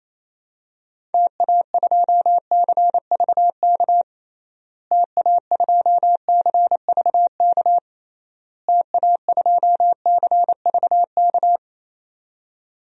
Morse code, heard as a keyed tone, about 28 wpm, 700 Hz. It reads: TA2CVK TA2CVK TA2CVK
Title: DXCC-100 Call Signs - 6 Characters - Repeated 3x - 28wpm